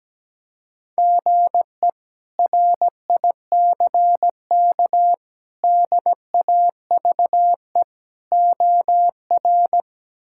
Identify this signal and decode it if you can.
Morse code, heard as a keyed tone, about 17 wpm, 700 Hz. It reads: GE RICK DAVE OR
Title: K1USN SST QSO Practice - 17wpm